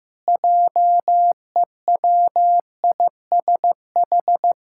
Morse code, heard as a keyed tone, about 15 wpm, 700 Hz. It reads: JEWISH
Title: Pangram Challenge v2 - 15wpm